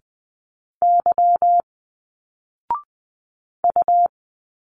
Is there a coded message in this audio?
Yes: Morse code from 0.8 to 4.1 s, heard as a keyed tone, about 20 wpm, 700 Hz. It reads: Y U